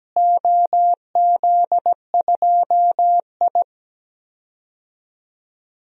Morse code, heard as a keyed tone, about 17 wpm, 700 Hz. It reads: OZ2I
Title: DXCC-100 Call Signs - 4 Characters - Repeated 3x - 17wpm